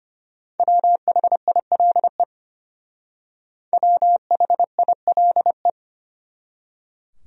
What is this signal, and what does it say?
Morse code, heard as a keyed tone, about 32 wpm, 700 Hz. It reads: WHILE WHILE